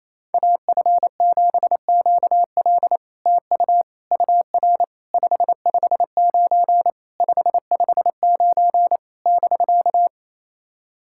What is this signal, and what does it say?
Morse code, heard as a keyed tone, about 28 wpm, 700 Hz. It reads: AF7QL TU UR 559 559 <BK>
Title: SOTA QSO Practice 28wpm